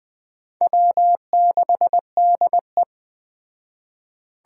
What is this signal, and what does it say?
Morse code, heard as a keyed tone, about 20 wpm, 700 Hz. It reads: W6DE
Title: Speed Racing - USA 1x2 Call Signs 17wpm